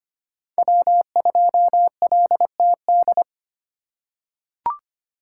Morse code, heard as a keyed tone, about 25 wpm, 700 Hz. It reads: W2LTD
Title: USA Call Signs - 2x Character Spacing - 25wpm